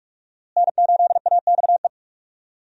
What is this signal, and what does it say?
Morse code, heard as a keyed tone, about 46 wpm, 700 Hz. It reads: N8AXE
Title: USA Call Signs - 5 Characters - Repeated 3x - 45wpm